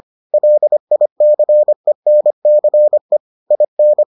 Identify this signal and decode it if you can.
Morse code, heard as a keyed tone, about 25 wpm, 600 Hz. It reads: LICENCE IN